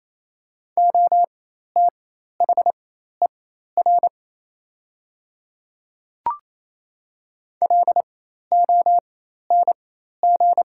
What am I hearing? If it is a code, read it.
Morse code, heard as a keyed tone, about 28 wpm, spaced as for 13 wpm, 700 Hz. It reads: OTHER LONG
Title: 4x Character Spacing - Top 100 Words - 28wpm